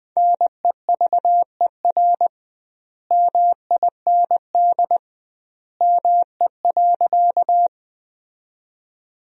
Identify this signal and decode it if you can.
Morse code, heard as a keyed tone, about 21 wpm, 700 Hz. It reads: NEVER MIND ME.
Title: 2x Word-Spacing - Sentences from Top 500 Words - 20wpm